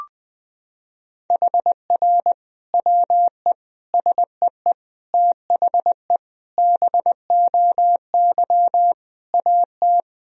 Morse code, heard as a keyed tone, about 20 wpm, 700 Hz. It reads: HR WE SEE THE BOY AT
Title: Sentences from Top 300 Words and Abbreviations 20wpm